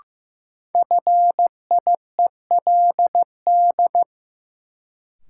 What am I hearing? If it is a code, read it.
Morse code, heard as a keyed tone, about 15 wpm, 700 Hz. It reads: FIELD
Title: Sentences from Top 1000 Words - New Word Drill - 15wpm